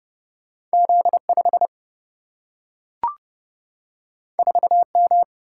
Morse code, heard as a keyed tone, about 30 wpm, 700 Hz. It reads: Z 5 4 M
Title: Single-Letter Single-Number 30wpm